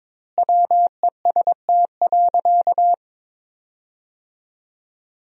Morse code, heard as a keyed tone, about 22 wpm, 700 Hz. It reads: WEST.
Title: 2x Word-Spacing - Sentences from Top 400 Words - 22wpm